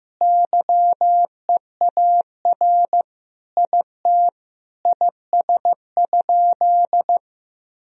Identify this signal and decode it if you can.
Morse code, heard as a keyed tone, about 15 wpm, 700 Hz. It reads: YEAR IT IS?
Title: Sentences from Top 1000 Words - Review Part 1 - 15wpm